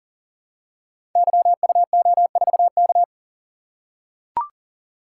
Morse code, heard as a keyed tone, about 41 wpm, 700 Hz. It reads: YUO4K
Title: Code Groups - 5 Letter:Numbers - 40wpm